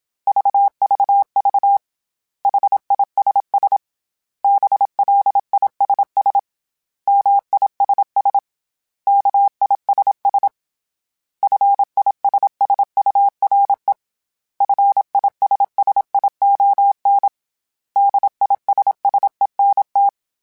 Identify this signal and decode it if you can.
Morse code, heard as a keyed tone, about 27 wpm, 800 Hz. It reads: VVV HISS BLISS MISS KISS FISSURE FISSION DISSENT
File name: Special INGs / Special INGs-mp3s / Suffixes @25